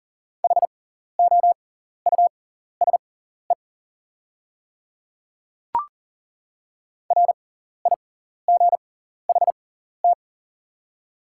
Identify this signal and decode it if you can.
Morse code, heard as a keyed tone, about 40 wpm, spaced as for 14 wpm, 690 Hz. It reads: HOUSE RIGHT
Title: 6x Character Spacing - Top 300 Words - 40wpm